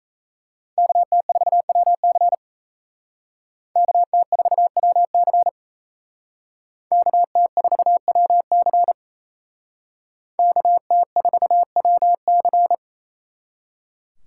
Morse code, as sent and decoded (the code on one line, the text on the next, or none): -.- - ....- .-- -.-. / -.- - ....- .-- -.-. / -.- - ....- .-- -.-. / -.- - ....- .-- -.-.
KT4WC KT4WC KT4WC KT4WC